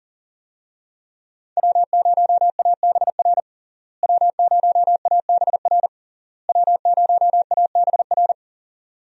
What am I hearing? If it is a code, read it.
Morse code, heard as a keyed tone, about 40 wpm, 700 Hz. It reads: W0ABR W0ABR W0ABR
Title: USA Call Signs - 5 Characters - Repeated 3x - 40wpm